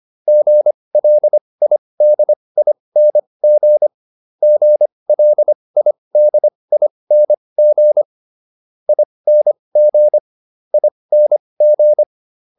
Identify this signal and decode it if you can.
Morse code, heard as a keyed tone, about 25 wpm, spaced as for 20 wpm, 600 Hz. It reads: GLIDING GLIDING ING ING